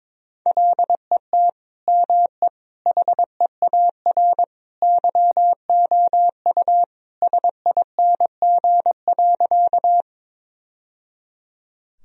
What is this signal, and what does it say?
Morse code, heard as a keyed tone, about 22 wpm, 700 Hz. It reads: LET ME HEAR YOU SING.
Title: Sentences from Top 1000 Words - Review Part 2 - 22wpm